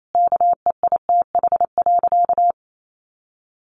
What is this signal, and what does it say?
Morse code, heard as a keyed tone, about 28 wpm, 700 Hz. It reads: KEITH.
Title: A Sound Of Thunder 28wpm